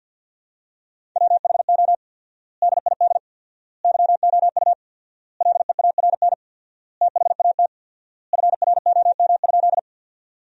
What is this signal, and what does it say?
Morse code, heard as a keyed tone, about 52 wpm, 700 Hz. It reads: WHO DID YOU LEARN THAT FROM?
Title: 4x Word-Spacing - Sentences from Top 300 Words - 50wpm